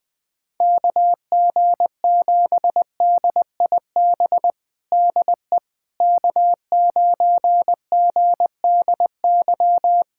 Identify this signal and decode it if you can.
Morse code, heard as a keyed tone, about 21 wpm, 700 Hz. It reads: KG7DIB DE K9GDY
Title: USA Callsign DE Callsign 20wpm